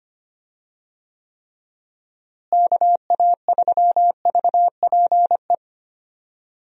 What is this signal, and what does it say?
Morse code, heard as a keyed tone, about 25 wpm, 700 Hz. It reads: KA3VPE
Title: USA Call Signs - 6 Characters - Repeated 3x - 25wpm